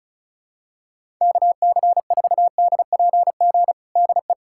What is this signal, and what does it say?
Morse code, heard as a keyed tone, about 36 wpm, 700 Hz. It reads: KC4DPG DE
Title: USA Callsign DE Callsign - 6 Characters - 35wpm